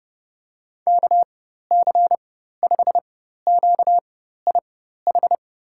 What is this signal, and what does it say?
Morse code, heard as a keyed tone, about 31 wpm, spaced as for 14 wpm, 700 Hz. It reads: KC5QIH
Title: USA Call Signs - 6 Characters - 4x Character Spacing - 30wpm